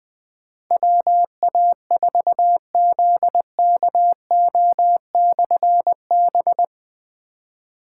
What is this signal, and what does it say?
Morse code, heard as a keyed tone, about 20 wpm, 700 Hz. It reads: WA4ZKO/B
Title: Difficult Call Signs 20wpm